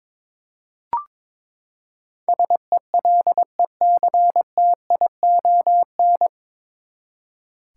Morse code, heard as a keyed tone, about 22 wpm, 700 Hz. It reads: SELECTION
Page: Morse Code Ninja practice 22wpm